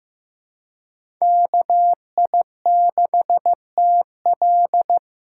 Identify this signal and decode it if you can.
Morse code, heard as a keyed tone, about 15 wpm, 700 Hz. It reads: KI6TL
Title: USA Call Signs - 5 Characters - 3x Character Spacing - 15wpm